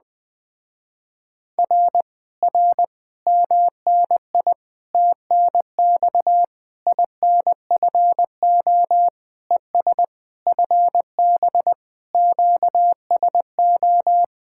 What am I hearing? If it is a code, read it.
Morse code, heard as a keyed tone, about 20 wpm, 700 Hz. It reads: R R MNI TNX INFO ES FB QSO